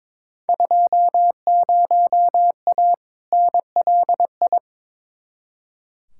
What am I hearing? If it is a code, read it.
Morse code, heard as a keyed tone, about 22 wpm, 700 Hz. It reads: 20A NLI
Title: Field Day Exchanges Encore - 22wpm